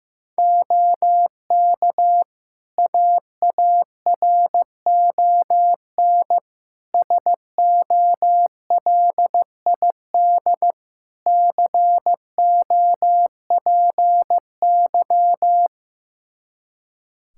Morse code, heard as a keyed tone, about 16 wpm, 700 Hz. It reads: OK AARON SOLID COPY
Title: K1USN SST QSO Practice 15wpm